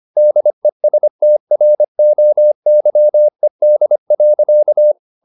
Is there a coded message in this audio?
Yes: Morse code from 0.2 to 4.9 s, heard as a keyed tone, about 25 wpm, 600 Hz. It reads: DESTROYED.